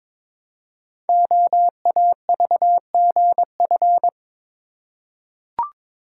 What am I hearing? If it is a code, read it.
Morse code, heard as a keyed tone, about 22 wpm, 700 Hz. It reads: OAVGF
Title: Code Groups - 5 Letters - 22wpm